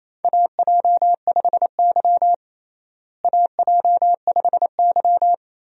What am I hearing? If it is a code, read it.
Morse code, heard as a keyed tone, about 28 wpm, 700 Hz. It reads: AJ5Y AJ5Y
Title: USA Call Signs - 4 Characters - Repeated 3x - 28wpm